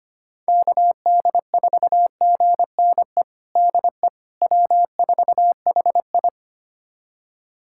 Morse code, heard as a keyed tone, about 25 wpm, 700 Hz. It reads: KD4GNE DE W4HI
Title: USA Callsign DE Callsign 25wpm